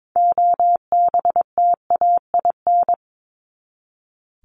Morse code, heard as a keyed tone, about 22 wpm, 700 Hz. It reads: OBTAIN
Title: ED Suffix - Top 100 Words with Base Words 22wpm